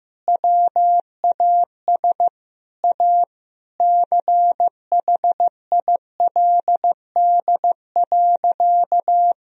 Morse code, heard as a keyed tone, about 15 wpm, 700 Hz. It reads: WAS A CHILD.
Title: Sentences from Top 1000 Words - Review Part 3 - 15wpm